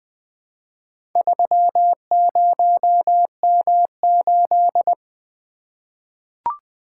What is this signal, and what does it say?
Morse code, heard as a keyed tone, about 20 wpm, 700 Hz. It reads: 30M8
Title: Four Letter-Number Combinations - 20wpm